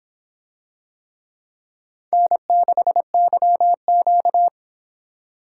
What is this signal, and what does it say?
Morse code, heard as a keyed tone, about 26 wpm, 700 Hz. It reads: N6YQ